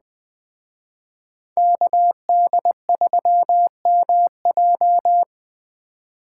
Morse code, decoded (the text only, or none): KD3MJ